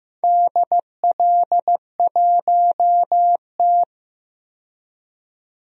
Morse code, heard as a keyed tone, about 15 wpm, 700 Hz. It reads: DL1T
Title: DXCC-100 Call Signs - 4 Characters - 15wpm